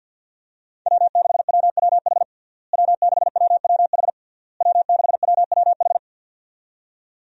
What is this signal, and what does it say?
Morse code, heard as a keyed tone, about 50 wpm, 700 Hz. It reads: W6WWH W6WWH W6WWH
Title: USA Call Signs - 5 Characters - Repeated 3x - 50wpm